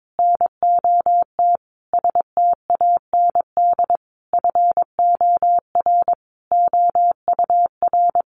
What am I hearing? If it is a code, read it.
Morse code, heard as a keyed tone, about 22 wpm, 700 Hz. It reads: NOT STAND FOR OUR
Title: Sentences from Top 1500 Words 22wpm